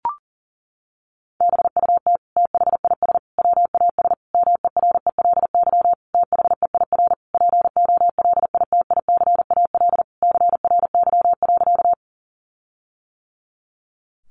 Morse code, heard as a keyed tone, about 40 wpm, 700 Hz. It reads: BUT THIS WAS MERELY THEIR POLITICAL CRY.
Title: Sentences from Top 1400 Words 40wpm